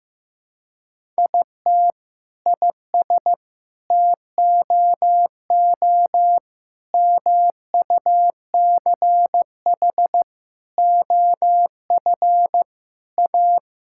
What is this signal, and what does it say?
Morse code, heard as a keyed tone, about 15 wpm, 700 Hz. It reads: IT IS TOO MUCH OF A
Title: Sentences from Top 1000 Words - Review Part 5 - 15wpm